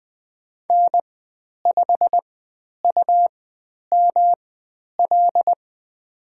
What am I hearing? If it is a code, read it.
Morse code, heard as a keyed tone, about 20 wpm, spaced as for 10 wpm, 700 Hz. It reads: N5UML
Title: 109 Lesson 18g - Letter m - Call Signs 20wpm